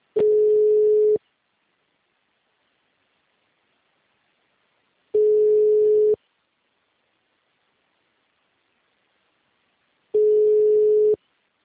Ringback tone (Europe, 425 Hz).